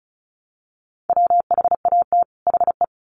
Morse code, heard as a keyed tone, about 34 wpm, 700 Hz. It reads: WHAT HE
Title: Sets of 2 Words 35wpm